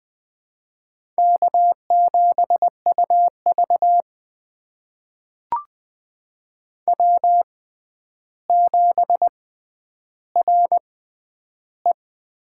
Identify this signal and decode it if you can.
Morse code, heard as a keyed tone, about 20 wpm, spaced as for 10 wpm, 700 Hz. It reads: K7UV W7RE
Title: USA Call Signs - 4 Characters - 6x Character Spacing - 20wpm